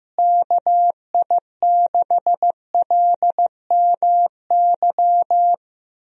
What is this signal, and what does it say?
Morse code, heard as a keyed tone, about 15 wpm, 700 Hz. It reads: KI6LMY
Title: USA Call Signs - 6 Characters - Repeated 3x - 15wpm